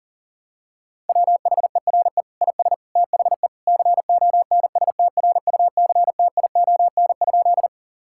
Morse code, heard as a keyed tone, about 40 wpm, 700 Hz. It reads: WHERE IS THE CONSTRUCTION?
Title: Sentences from Top 1600 Words 40wpm